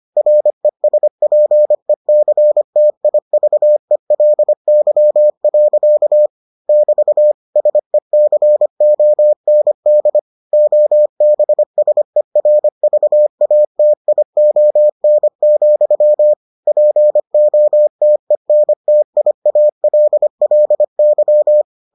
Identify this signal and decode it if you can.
Morse code, heard as a keyed tone, about 25 wpm, 600 Hz. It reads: RESPECTIVELY. <BT> SECOND OBSERVATION, POTENTIALLY